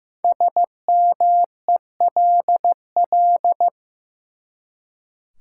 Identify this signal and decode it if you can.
Morse code, heard as a keyed tone, about 15 wpm, 700 Hz. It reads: SMELL
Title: Five Letter Words - 15wpm